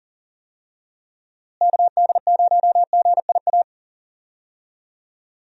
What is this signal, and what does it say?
Morse code, heard as a keyed tone, about 37 wpm, 700 Hz. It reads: KD0GIA
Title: USA Call Signs - 6 Characters - Repeated 3x - 40wpm